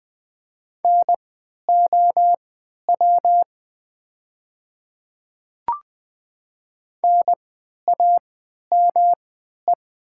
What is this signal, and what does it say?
Morse code, heard as a keyed tone, about 20 wpm, spaced as for 11 wpm, 700 Hz. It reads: NOW NAME
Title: 3x Character Spacing - Top 200 Words - 20wpm